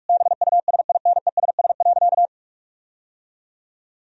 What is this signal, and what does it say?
Morse code, heard as a keyed tone, about 46 wpm, 700 Hz. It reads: BUSINESS.
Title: Sentences from Top 1000 Words - Review Part 4 - 45wpm